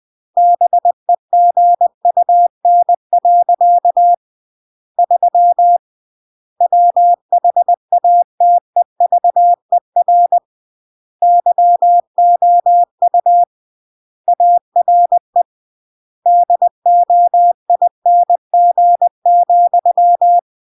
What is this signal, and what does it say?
Morse code, heard as a keyed tone, about 20 wpm, 700 Hz. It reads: BEGUN. 3 WHATEVER YOU ARE DOING,